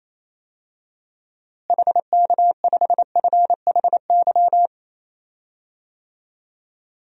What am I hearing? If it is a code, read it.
Morse code, heard as a keyed tone, about 28 wpm, 700 Hz. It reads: HK5FHY
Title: DXCC-100 Call Signs - 6 Characters - Repeated 2x - 28wpm